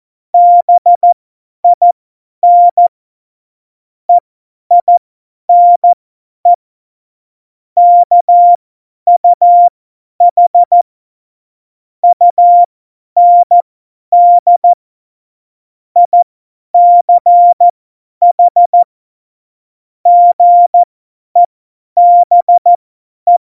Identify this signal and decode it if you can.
Morse code, heard as a keyed tone, about 14 wpm, spaced as for 10 wpm, 700 Hz. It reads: BIN EINE KUH UND ICH GEBE
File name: Morsekurs / Witze / Witz 15. Dezember 2020, 50 F70